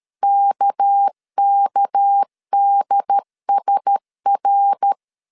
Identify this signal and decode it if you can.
Morse code, heard as a keyed tone, about 13 wpm, 800 Hz. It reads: KKDSR
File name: LEZIONI / 6 PKX0 / 6 PK 12wpm 2006